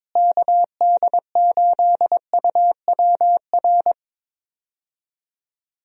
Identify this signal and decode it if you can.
Morse code, heard as a keyed tone, about 22 wpm, 700 Hz. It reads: KD8UWR
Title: USA Call Signs - 6 Characters - Repeated 2x - 22wpm